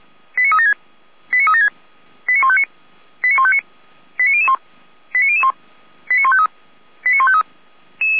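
Five-tone selective call (ZVEI), heard as quick runs of five tones, tones of about 70 ms. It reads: 89376 89376 89159 89159 89001 89001 89153 89153
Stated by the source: ZVEI five-tone call 89150 received by SDR.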